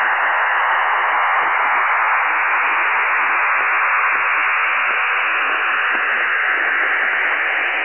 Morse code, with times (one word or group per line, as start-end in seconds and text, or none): none